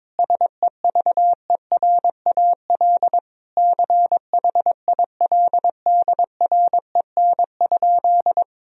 Morse code, heard as a keyed tone, about 22 wpm, 700 Hz. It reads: SEVERAL CHILDREN?
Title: Sentences from Top 1000 Words - Review Part 2 - 22wpm